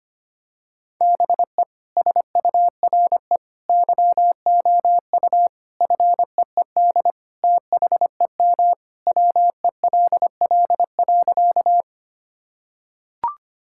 Morse code, heard as a keyed tone, about 25 wpm, 700 Hz. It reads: BE SURE YOU FEED THEM WELL.
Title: Sentences from Top 1600 Words 25wpm